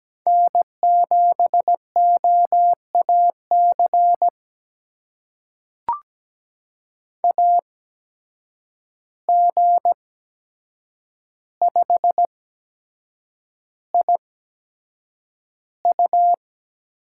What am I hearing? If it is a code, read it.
Morse code, heard as a keyed tone, about 17 wpm, spaced as for 7 wpm, 700 Hz. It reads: N7OAC AG5IU
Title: USA Call Signs - 8x Character Spacing - 17wpm